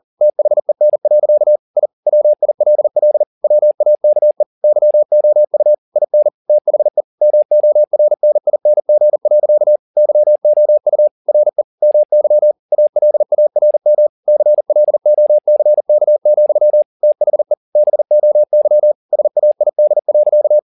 Morse code, heard as a keyed tone, about 40 wpm, 600 Hz. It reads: THEN. I WILL WAKE YOU IN THE MORNING. YOU RE MY ALARM CLOCK, THE BOY SAID.